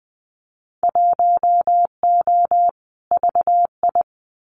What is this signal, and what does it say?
Morse code, heard as a keyed tone, about 20 wpm, 700 Hz. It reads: 1O VI
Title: Winter Field Day Exchanges Encore 20wpm